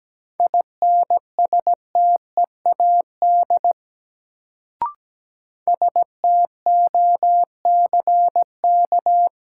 Morse code, heard as a keyed tone, about 17 wpm, 700 Hz. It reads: INSTEAD STOCK
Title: Sentences from Top 800 Words - New Word Drill - 17wpm